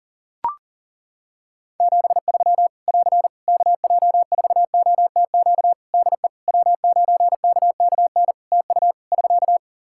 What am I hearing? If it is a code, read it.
Morse code, heard as a keyed tone, about 40 wpm, 700 Hz. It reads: 73 <AR> KJ4OTQ DE W9KKN TU <SK>
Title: Practice QSO - Exchange 4 - Ending - 40wpm